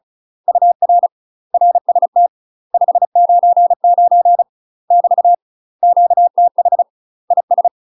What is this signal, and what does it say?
Morse code, heard as a keyed tone, about 34 wpm, 700 Hz. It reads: UR RST 599 <BT> QTH IS
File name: UK MASTER QSOs TRAINING / 35WPM / UK MASTER QSOs 0002